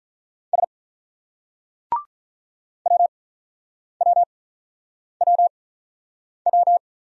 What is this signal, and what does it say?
Morse code, heard as a keyed tone, about 46 wpm, spaced as for 9 wpm, 700 Hz. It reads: S WWWW